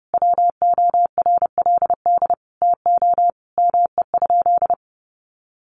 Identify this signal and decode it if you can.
Morse code, heard as a keyed tone, about 28 wpm, 700 Hz. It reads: WORLD TO ME?
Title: Sentences from Top 900 Words 30wpm